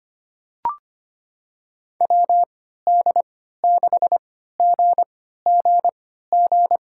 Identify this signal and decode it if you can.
Morse code, heard as a keyed tone, about 25 wpm, spaced as for 14 wpm, 700 Hz. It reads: WD6GGG
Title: USA Call Signs - 6 Characters - 3x Character Spacing - 25wpm